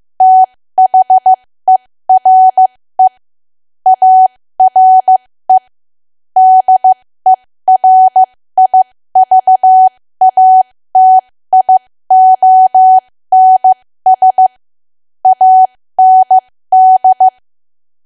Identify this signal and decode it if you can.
Morse code, heard as a keyed tone, about 15 wpm, spaced as for 13 wpm, 750 Hz. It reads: THERE ARE DERIVATIONS AND
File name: cwa / W1AW-Level2 / 13WPM